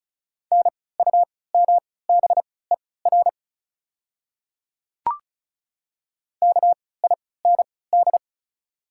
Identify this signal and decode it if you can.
Morse code, heard as a keyed tone, about 36 wpm, spaced as for 20 wpm, 690 Hz. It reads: NUMBER KIND
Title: 3x Character Spacing - Top 200 Words - 35wpm